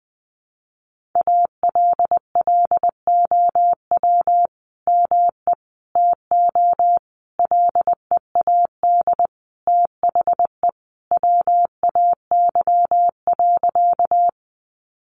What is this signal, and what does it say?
Morse code, heard as a keyed tone, about 20 wpm, 700 Hz. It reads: ALLOW ME TO LEAD THE WAY.
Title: Sentences from Top 700 Words 20wpm